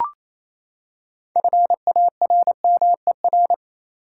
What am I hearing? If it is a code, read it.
Morse code, heard as a keyed tone, about 28 wpm, 700 Hz. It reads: FARMER
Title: Six Letter Words - 28wpm